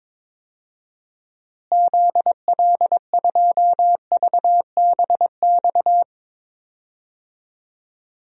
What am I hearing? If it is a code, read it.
Morse code, heard as a keyed tone, about 22 wpm, 700 Hz. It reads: ZL2VBX
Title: DXCC-100 Call Signs - 6 Characters - Repeated 2x - 22wpm